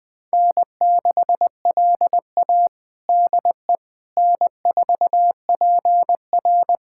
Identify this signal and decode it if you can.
Morse code, heard as a keyed tone, about 20 wpm, 700 Hz. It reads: N6LA DE N4PR
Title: USA Callsign DE Callsign - 4 Characters - 20wpm